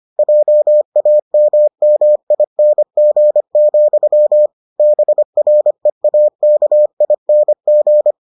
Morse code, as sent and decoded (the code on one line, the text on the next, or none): .--- .- -- -- .. -. --. --..-- / -... .-. .- -.- .. -. --.
JAMMING, BREAKING